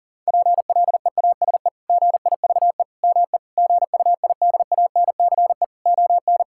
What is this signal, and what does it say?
Morse code, heard as a keyed tone, about 37 wpm, 700 Hz. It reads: PLEASE GIVE ME GUIDANCE ON
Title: Sentences from Top 1600 Words 40wpm